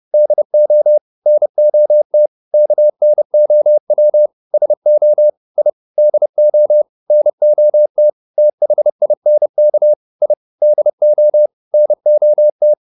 Morse code, heard as a keyed tone, about 30 wpm, 600 Hz. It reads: DO NOT KNOW SO I DO NOT THINK I DO NOT